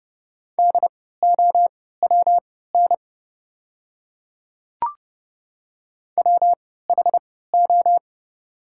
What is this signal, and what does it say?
Morse code, heard as a keyed tone, about 30 wpm, spaced as for 17 wpm, 700 Hz. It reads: DOWN WHO
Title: 3x Character Spacing - Top 100 Words - 30wpm